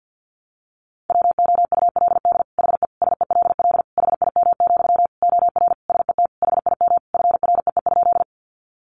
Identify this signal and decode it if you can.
Morse code, heard as a keyed tone, about 50 wpm, 700 Hz. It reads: WOULD HE SELL HIM, OR SET HIM FREE?
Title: Sentences from Top 1100 Words 50wpm